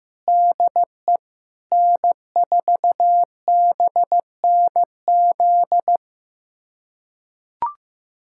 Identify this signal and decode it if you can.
Morse code, heard as a keyed tone, about 15 wpm, 700 Hz. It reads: DE N4BNZ
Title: USA Callsign DE Callsign - 5 Characters - 15wpm